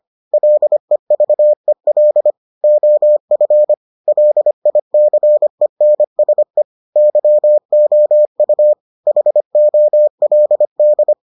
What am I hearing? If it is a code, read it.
Morse code, heard as a keyed tone, about 25 wpm, 600 Hz. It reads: LEVEL OF LICENSE YOU HOLD